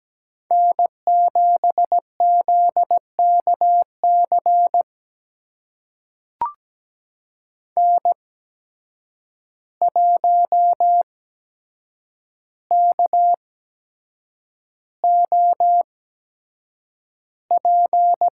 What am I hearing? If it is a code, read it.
Morse code, heard as a keyed tone, about 17 wpm, spaced as for 7 wpm, 700 Hz. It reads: N7ZKC N1KOP
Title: USA Call Signs - 5 Characters - 8x Character Spacing - 17wpm